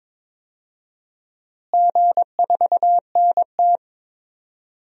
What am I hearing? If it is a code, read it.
Morse code, heard as a keyed tone, about 22 wpm, 700 Hz. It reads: G4NT
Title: DXCC-100 Call Signs - 4 Characters - Repeated 2x - 22wpm